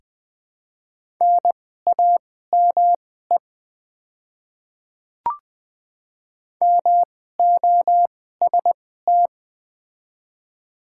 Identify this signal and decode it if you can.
Morse code, heard as a keyed tone, about 20 wpm, spaced as for 15 wpm, 700 Hz. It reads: NAME MOST
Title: 2x Character Spacing - Top 200 Words - 20wpm